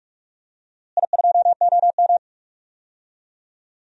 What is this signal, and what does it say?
Morse code, heard as a keyed tone, about 42 wpm, 690 Hz. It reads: I2OM
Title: DXCC-100 Call Signs - 4 Characters - Repeated 3x - 45wpm